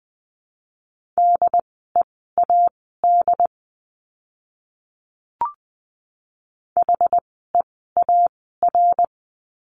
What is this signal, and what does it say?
Morse code, heard as a keyed tone, about 20 wpm, spaced as for 14 wpm, 700 Hz. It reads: DEAD HEAR
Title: 075 Lesson 13f - Letter L - Words 20wpm